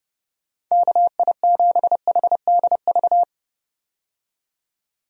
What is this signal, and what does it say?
Morse code, heard as a keyed tone, about 31 wpm, 700 Hz. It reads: KI7HDV
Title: USA Call Signs - 6 Characters - 30wpm